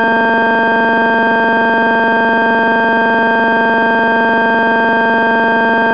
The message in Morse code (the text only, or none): none